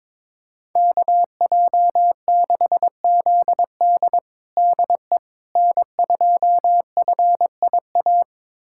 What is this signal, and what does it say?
Morse code, heard as a keyed tone, about 22 wpm, 700 Hz. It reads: KJ6ZD DE N2FIA
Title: USA Callsign DE Callsign - 5 Characters - 22wpm